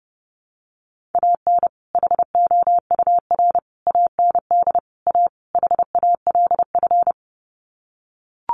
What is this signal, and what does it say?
Morse code, heard as a keyed tone, about 30 wpm, 700 Hz. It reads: AN HOUR AND A HALF
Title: Sets of 5 Words 30wpm